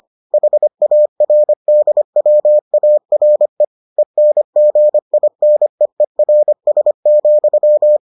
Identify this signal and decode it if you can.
Morse code, heard as a keyed tone, about 25 wpm, 600 Hz. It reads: HARDWARE ENGINEERS,